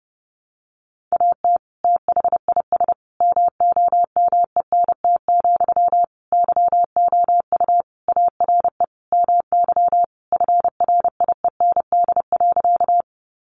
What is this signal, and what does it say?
Morse code, heard as a keyed tone, about 30 wpm, 700 Hz. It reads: AT THIS MOMENT, YOU ARE MY FRIEND.
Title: Sentences from Top 1000 Words 30wpm